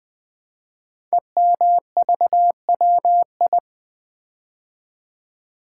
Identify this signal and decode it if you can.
Morse code, heard as a keyed tone, about 20 wpm, 700 Hz. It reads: EMVWI
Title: Code Groups - 5 Letters - 20wpm